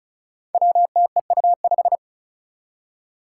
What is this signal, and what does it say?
Morse code, heard as a keyed tone, about 36 wpm, 700 Hz. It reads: WTEU5
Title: Five Challenging Letter-Number Combinations 35wpm